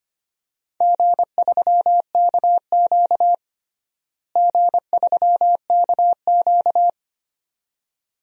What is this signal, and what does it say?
Morse code, heard as a keyed tone, about 25 wpm, 700 Hz. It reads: G3KQ G3KQ
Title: DXCC-100 Call Signs - 4 Characters - Repeated 2x - 25wpm